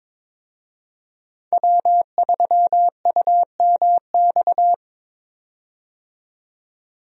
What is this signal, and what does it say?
Morse code, heard as a keyed tone, about 22 wpm, 700 Hz. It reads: W3UMX
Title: USA Call Signs - 5 Characters - 22wpm